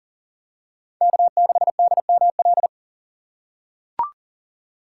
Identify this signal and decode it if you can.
Morse code, heard as a keyed tone, about 40 wpm, 690 Hz. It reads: K6DML
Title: USA Call Signs - 4x Character Spacing - 40wpm